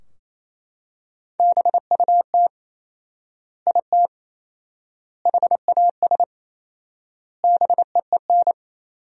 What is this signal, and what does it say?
Morse code, heard as a keyed tone, about 28 wpm, 700 Hz. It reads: BUT IT HAS BEEN
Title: 4x Word-Spacing - Sentences from Top 100 Words - 28wpm